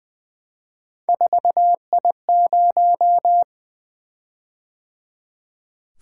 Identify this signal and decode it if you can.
Morse code, heard as a keyed tone, about 20 wpm, 700 Hz. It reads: 4I0
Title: Three Letter-Number Combinations - 20wpm